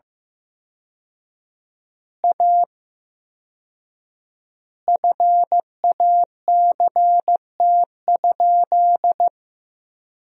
Morse code, heard as a keyed tone, about 15 wpm, 700 Hz. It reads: A FACT?
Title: 4x Word-Spacing - Sentences from Top 500 Words - 15wpm